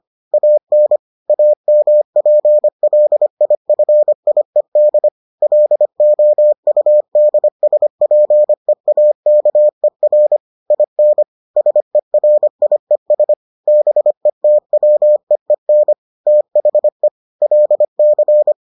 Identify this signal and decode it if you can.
Morse code, heard as a keyed tone, about 25 wpm, 600 Hz. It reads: AN AMPLIFIED LOUDSPEAKER IN SERIES BETWEEN THE LC